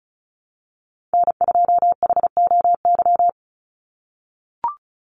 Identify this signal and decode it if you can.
Morse code, heard as a keyed tone, about 34 wpm, 700 Hz. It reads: N2HOY
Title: USA Call Signs 35wpm